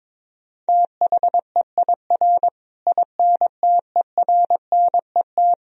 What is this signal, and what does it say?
Morse code, heard as a keyed tone, about 22 wpm, 700 Hz. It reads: THEIR INTERNET
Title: Sentences from Top 1000 Words - Review Part 4 - 22wpm